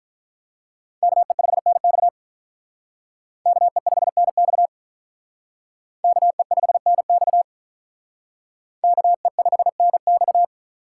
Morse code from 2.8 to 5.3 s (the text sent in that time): KE5NX